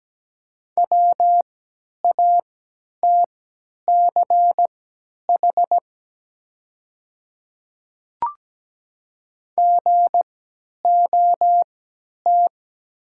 Morse code, heard as a keyed tone, about 17 wpm, spaced as for 10 wpm, 700 Hz. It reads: WATCH GOT
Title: 3x Character Spacing - Top 300 Words - 17wpm